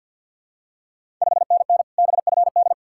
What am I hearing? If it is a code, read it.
Morse code, heard as a keyed tone, about 52 wpm, 700 Hz. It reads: 5NN BFD
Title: Ontario QSO Party Exchange 50wpm